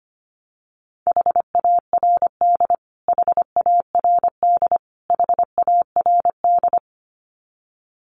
Morse code, heard as a keyed tone, about 25 wpm, 700 Hz. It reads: HARD HARD HARD
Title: Top 500 Words Repeated 25wpm